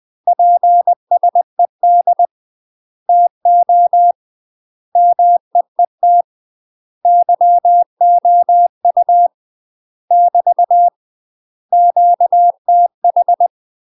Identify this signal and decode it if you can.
Morse code, heard as a keyed tone, about 20 wpm, 700 Hz. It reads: PSED TO MEET YOU <BT> QTH
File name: UK MASTER QSOs TRAINING/20WPM